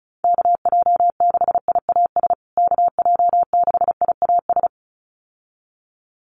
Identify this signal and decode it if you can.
Morse code, heard as a keyed tone, about 34 wpm, 700 Hz. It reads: KJ6IAS KJ6IAS
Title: USA Call Signs Repeated 2x 35wpm